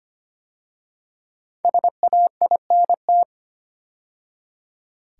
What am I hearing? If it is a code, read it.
Morse code, heard as a keyed tone, about 25 wpm, 700 Hz. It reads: SAINT
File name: 5 Letter Words - First 3 Letters Given - 25wpm